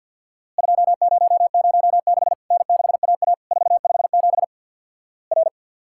Morse code, heard as a keyed tone, about 50 wpm, 700 Hz. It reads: 2006 N6AA 457 R
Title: Worked All Europe DX Contest - QTC Exchange with Repeats - 50wpm